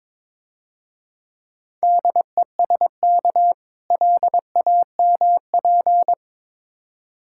Morse code, heard as a keyed tone, about 22 wpm, 700 Hz. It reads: DESK LAMP
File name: Easy Words - Speed-Racing with Hints - 22wpm